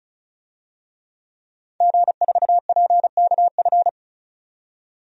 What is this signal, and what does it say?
Morse code, heard as a keyed tone, about 36 wpm, 700 Hz. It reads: G4PKF
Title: DXCC-100 Call Signs - 5 Characters - 35wpm